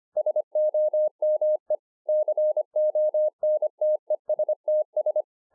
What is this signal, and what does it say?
Morse code, heard as a keyed tone, about 25 wpm, 610 Hz. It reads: SOME CONTESTS